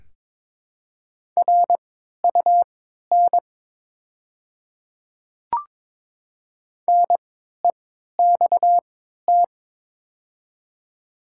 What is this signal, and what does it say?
Morse code, heard as a keyed tone, about 23 wpm, spaced as for 13 wpm, 700 Hz. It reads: RUN NEXT
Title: 3x Character Spacing - Top 300 Words - 22wpm